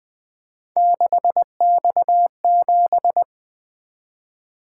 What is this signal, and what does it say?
Morse code, heard as a keyed tone, about 23 wpm, 700 Hz. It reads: 6X7
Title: Speed-Racing - Three Letter-Number Combinations - 20wpm